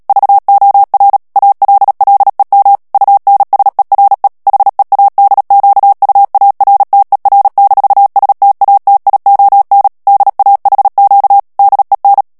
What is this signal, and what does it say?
Morse code, heard as a keyed tone, about 37 wpm, 800 Hz. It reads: VOR ALLEM UNSERE HEADQUARTER-STATION DAHQ DEN
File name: morse mp3 / ALLEMAND texte clair 2 vitesse 35 mots minute